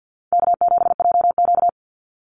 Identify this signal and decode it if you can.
Morse code, heard as a keyed tone, about 50 wpm, 700 Hz. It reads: K7JQ
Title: Easy Call Signs - 50wpm